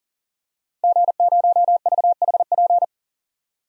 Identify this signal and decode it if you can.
Morse code, heard as a keyed tone, about 40 wpm, 700 Hz. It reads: G0VHP
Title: DXCC-100 Call Signs - 5 Characters - 40wpm